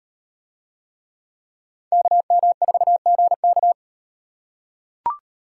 Morse code, heard as a keyed tone, about 38 wpm, 700 Hz. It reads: KM4GK